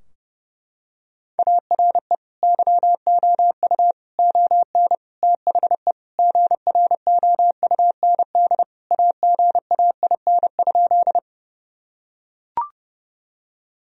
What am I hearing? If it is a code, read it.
Morse code, heard as a keyed tone, about 30 wpm, 700 Hz. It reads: ARE YOU ON THE GROUND AGAIN?
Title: Sentences from Top 1000 Words - Review Part 2 - 30wpm